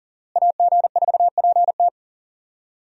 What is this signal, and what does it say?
Morse code, heard as a keyed tone, about 40 wpm, 700 Hz. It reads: AG4PT
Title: USA Call Signs - 5 Characters - Repeated 2x - 40wpm